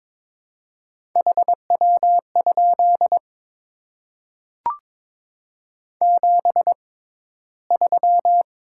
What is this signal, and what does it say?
Morse code, heard as a keyed tone, about 22 wpm, spaced as for 14 wpm, 700 Hz. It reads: HW? 73
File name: Top 20 QSO Elements - 6x Character Spacing - 22wpm